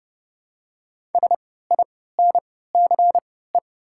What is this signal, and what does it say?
Morse code, heard as a keyed tone, about 31 wpm, spaced as for 17 wpm, 700 Hz. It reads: SINCE
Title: 3x Character Spacing - Top 300 Words - 30wpm